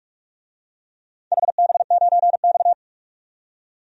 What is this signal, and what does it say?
Morse code, heard as a keyed tone, about 46 wpm, 700 Hz. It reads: HB9X
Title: DXCC-100 Call Signs - 4 Characters - 45wpm